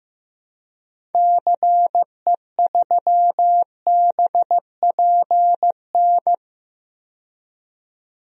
Morse code, heard as a keyed tone, about 16 wpm, 700 Hz. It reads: CE3BPN
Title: DXCC-100 Call Signs - 6 Characters - 15wpm